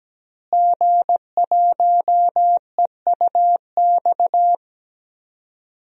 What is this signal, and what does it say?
Morse code, heard as a keyed tone, about 17 wpm, 700 Hz. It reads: G1EUX
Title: DXCC-100 Call Signs - 5 Characters - Repeated 2x - 17wpm